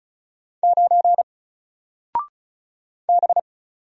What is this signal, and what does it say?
Morse code, heard as a keyed tone, about 36 wpm, 700 Hz. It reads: 9 B
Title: Single Letter-Number 35wpm